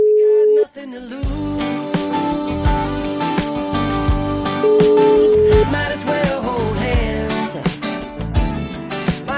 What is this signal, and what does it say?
Ringback tone (Europe, 425 Hz), with noise.